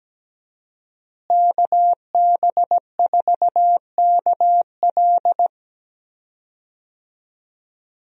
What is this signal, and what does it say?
Morse code, heard as a keyed tone, about 17 wpm, 700 Hz. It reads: KB4KL
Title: USA Call Signs - 5 Characters - Repeated 3x - 17wpm